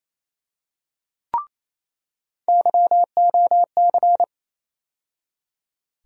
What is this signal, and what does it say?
Morse code, heard as a keyed tone, about 28 wpm, 700 Hz. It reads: YOC
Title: USA Call Sign Postfixes - 3 Letters 28wpm